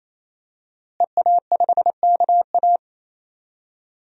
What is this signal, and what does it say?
Morse code, heard as a keyed tone, about 28 wpm, 700 Hz. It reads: EA5KA
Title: DXCC-100 Call Signs - 5 Characters - Repeated 3x - 28wpm